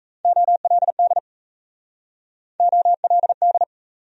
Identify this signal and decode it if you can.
Morse code, heard as a keyed tone, about 37 wpm, 700 Hz. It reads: OLD OLD